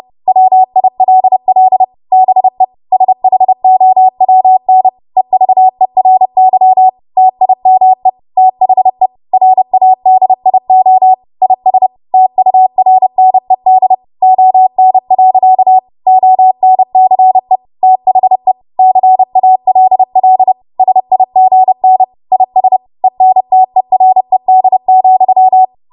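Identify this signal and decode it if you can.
Morse code, heard as a keyed tone, about 30 wpm, 750 Hz. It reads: WILL BE SHOWN EVERY TIME THE RADIO IS TURNED ON. ONCE THE CALL SIGN IS ENTERED,
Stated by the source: Morse practice transmission at 30 WPM.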